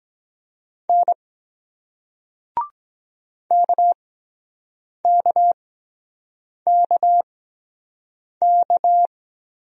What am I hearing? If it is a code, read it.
Morse code, heard as a keyed tone, about 22 wpm, spaced as for 7 wpm, 700 Hz. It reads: N KKKK